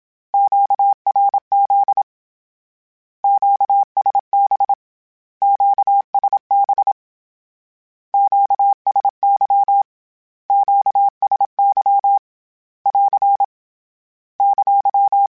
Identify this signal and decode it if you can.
Morse code, heard as a keyed tone, about 27 wpm, 800 Hz. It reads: QRZ QSB QSB QSY QSY <AR> !
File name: Q Calls / Q Calls-mp3s / Q Calls Twice @25